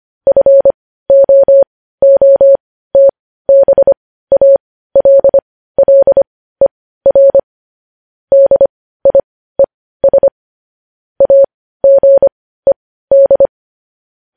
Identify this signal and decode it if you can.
Morse code, heard as a keyed tone, about 25 wpm, spaced as for 15 wpm, 550 Hz. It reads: FOOTBALLER DIES AGED
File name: podcast / News headlines-1x-25x15